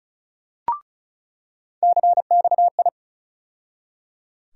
Morse code, heard as a keyed tone, about 34 wpm, 700 Hz. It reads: CXI
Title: USA Call Sign Postfixes - 3 Letters 35wpm